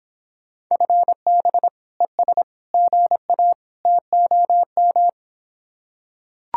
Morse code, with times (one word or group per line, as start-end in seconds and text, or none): 0.7-1.7 FB
2.0-2.4 ES
2.7-3.5 GA
3.9-5.1 TOM